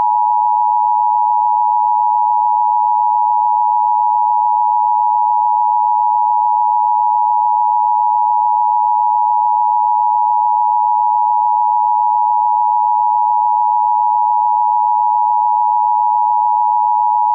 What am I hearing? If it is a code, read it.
Emergency Alert System (SAME) tones (the 853 and 960 Hz attention tone). It is heard from 1.1 to 16.8 s.